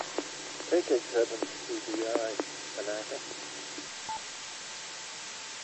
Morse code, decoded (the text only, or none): none